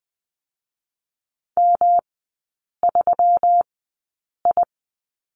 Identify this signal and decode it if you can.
Morse code, heard as a keyed tone, about 20 wpm, 700 Hz. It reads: M 3 I
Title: 121 Lesson 20e - Number 3 - Three Characters 20wpm